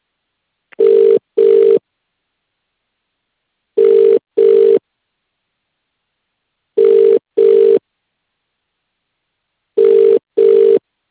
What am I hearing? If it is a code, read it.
Ringback tone (UK), 400 + 450 Hz.